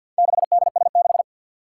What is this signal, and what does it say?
Morse code, heard as a keyed tone, about 52 wpm, 700 Hz. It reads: 6DS6
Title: Four Challenging Letter-Number Combinations 50wpm